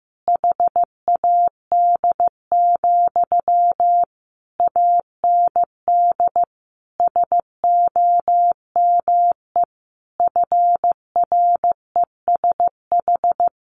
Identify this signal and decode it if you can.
Morse code, heard as a keyed tone, about 15 wpm, 700 Hz. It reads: HAD, AND SOME FRESH
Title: Sentences from Top 1500 Words 15wpm